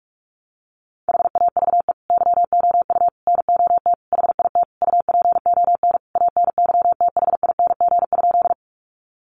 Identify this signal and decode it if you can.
Morse code, heard as a keyed tone, about 44 wpm, 700 Hz. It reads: HAVE YOU NOT HIT UPON ANYTHING?
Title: Sentences from Top 1200 Words 45wpm